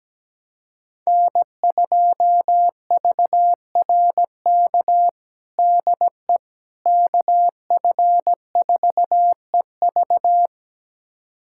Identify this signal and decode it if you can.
Morse code, heard as a keyed tone, about 17 wpm, 700 Hz. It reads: N2VRK DE KF4EV
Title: USA Callsign DE Callsign - 5 Characters - 17wpm